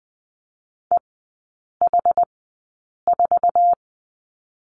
Morse code, heard as a keyed tone, about 20 wpm, spaced as for 8 wpm, 700 Hz. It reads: EH4
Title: 074 Lesson 13e - Letter L - Three Characters 20wpm